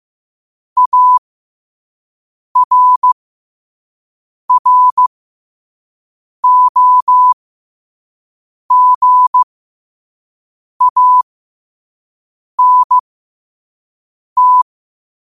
Morse code, heard as a keyed tone, about 15 wpm, spaced as for 5 wpm, 1000 Hz. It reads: ARROGANT
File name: qotdfiles / 5wpm